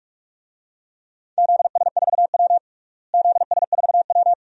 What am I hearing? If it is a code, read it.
Morse code, heard as a keyed tone, about 43 wpm, 700 Hz. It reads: ZS4W ZS4W
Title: DXCC-100 Call Signs - 4 Characters - Repeated 2x - 45wpm